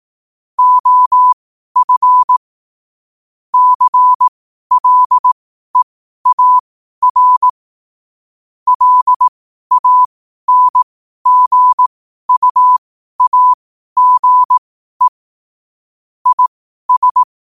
Morse code, heard as a keyed tone, about 18 wpm, spaced as for 12 wpm, 1000 Hz. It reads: OF CLEAR LANGUAGE IS
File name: qotdfiles / 12wpm